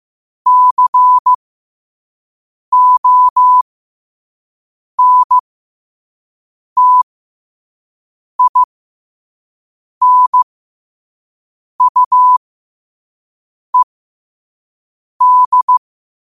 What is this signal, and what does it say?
Morse code, heard as a keyed tone, about 15 wpm, spaced as for 5 wpm, 1000 Hz. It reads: CONTINUED
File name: qotdfiles / 5wpm